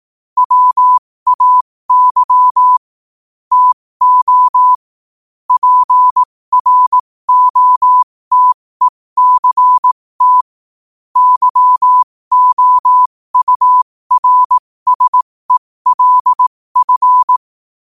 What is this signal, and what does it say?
Morse code, heard as a keyed tone, about 18 wpm, spaced as for 15 wpm, 1000 Hz. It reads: WAY TO PROTECT YOURSELF